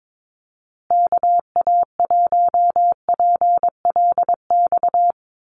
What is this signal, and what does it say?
Morse code, heard as a keyed tone, about 22 wpm, 700 Hz. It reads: KA1PLX
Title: USA Call Signs Repeated 3x 22wpm